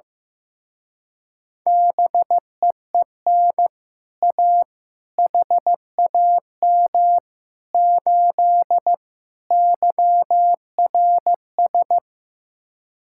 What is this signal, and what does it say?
Morse code, heard as a keyed tone, about 15 wpm, 700 Hz. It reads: BEEN A HAM 8 YRS
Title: Speed-Racing Common QSO Phrases Encore - 15wpm